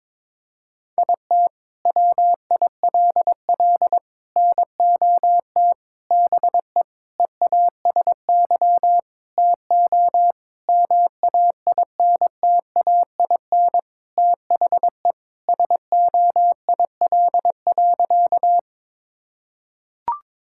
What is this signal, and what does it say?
Morse code, heard as a keyed tone, about 22 wpm, 700 Hz. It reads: IT WILL NOT BE EASY TO MAINTAIN THE SOIL.
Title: Sentences from Top 1000 Words - Review Part 5 - 22wpm